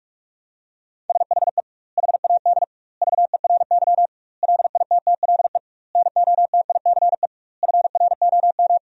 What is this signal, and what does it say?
Morse code, heard as a keyed tone, about 46 wpm, 700 Hz. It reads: SHE HAD VERY LITTLE NOTICE FROM
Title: 2x Word-Spacing - Sentences from Top 400 Words - 45wpm